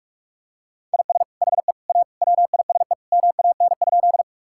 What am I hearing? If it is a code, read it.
Morse code, heard as a keyed tone, about 48 wpm, 700 Hz. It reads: IS HE A WISE MAN?
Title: Sentences from Top 1600 Words 45wpm